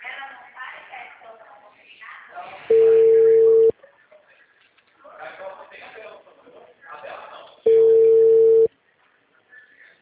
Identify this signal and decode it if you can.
Ringback tone (China, 450 Hz), with speech.